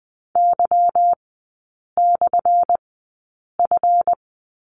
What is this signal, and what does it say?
Morse code, heard as a keyed tone, about 20 wpm, spaced as for 8 wpm, 700 Hz. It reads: Y/F
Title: 178 Lesson 29e - Slash - Three Characters 20wpm